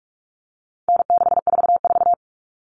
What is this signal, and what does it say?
Morse code, heard as a keyed tone, about 44 wpm, 700 Hz. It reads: N644
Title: Common Cut Numbers - 4-Digit 45wpm